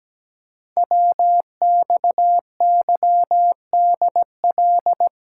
Morse code, heard as a keyed tone, about 17 wpm, 700 Hz. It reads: WXYDL
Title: Code Groups - 5 Letters - 17wpm